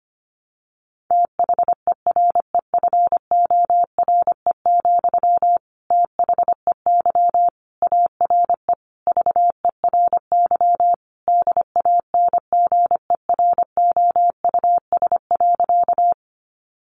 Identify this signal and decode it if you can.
Morse code, heard as a keyed tone, about 25 wpm, 700 Hz. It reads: THEREFORE, THEY ARE VERY DANGEROUS.
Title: Sentences from Top 1200 Words 25wpm